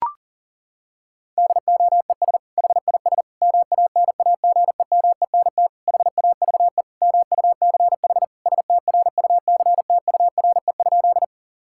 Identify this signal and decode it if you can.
Morse code, heard as a keyed tone, about 40 wpm, 700 Hz. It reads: DOES HIS MANAGEMENT HAVE MUCH STRUCTURE?
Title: Sentences from Top 1000 Words - Review Part 6 - 40wpm